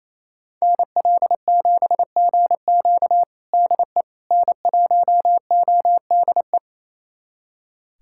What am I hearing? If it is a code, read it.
Morse code, heard as a keyed tone, about 28 wpm, 700 Hz. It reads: NL7GQ DE N1ODE
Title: USA Callsign DE Callsign - 5 Characters - 28wpm